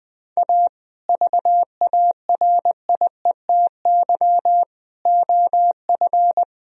Morse code, heard as a keyed tone, about 20 wpm, 700 Hz. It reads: A VARIETY OF
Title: Speed-Racing - Sets of 3 Words 17wpm